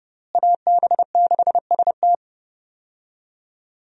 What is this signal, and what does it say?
Morse code, heard as a keyed tone, about 32 wpm, 700 Hz. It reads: AB6ST
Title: USA Call Signs - 5 Characters - Repeated 3x - 30wpm